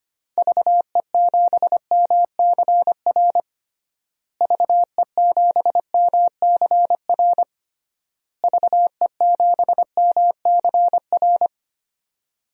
Morse code, heard as a keyed tone, about 25 wpm, 700 Hz. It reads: VE7MCR VE7MCR VE7MCR
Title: DXCC-100 Call Signs - 6 Characters - Repeated 3x - 25wpm